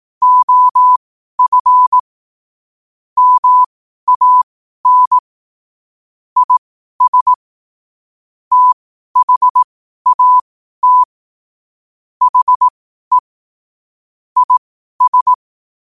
Morse code, heard as a keyed tone, about 18 wpm, spaced as for 12 wpm, 1000 Hz. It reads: OF MAN IS THAT HE IS
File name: qotdfiles / 12wpm